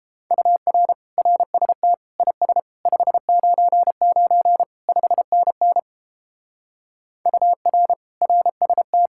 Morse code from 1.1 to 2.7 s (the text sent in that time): RST IS